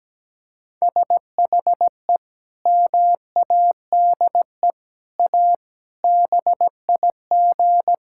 Morse code, heard as a keyed tone, about 17 wpm, 700 Hz. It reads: SHE MADE A BIG
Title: Sentences from Top 1000 Words - Review Part 6 - 17wpm